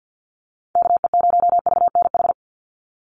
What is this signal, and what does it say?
Morse code, heard as a keyed tone, about 49 wpm, 700 Hz. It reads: KE0VNH
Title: USA Call Signs 50wpm